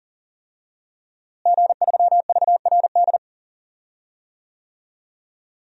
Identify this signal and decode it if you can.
Morse code, heard as a keyed tone, about 40 wpm, 700 Hz. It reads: G3VRD
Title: DXCC-100 Call Signs - 5 Characters - Repeated 2x - 40wpm